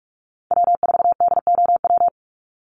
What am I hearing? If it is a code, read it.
Morse code, heard as a keyed tone, about 44 wpm, 700 Hz. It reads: W4DOW
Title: NA SOTA Chaser Callsigns 45wpm